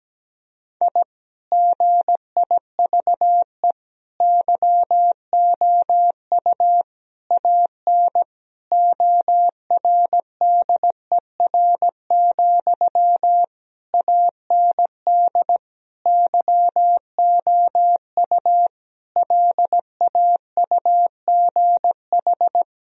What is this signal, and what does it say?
Morse code, heard as a keyed tone, about 17 wpm, 700 Hz. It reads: I GIVE YOU AN ORDER, AND YOU LAUGH
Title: Sentences from Top 1000 Words - Review Part 2 - 17wpm